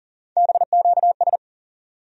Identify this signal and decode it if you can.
Morse code, heard as a keyed tone, about 40 wpm, 700 Hz. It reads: BQS
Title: USA Call Sign Postfixes - 3 Letters 40wpm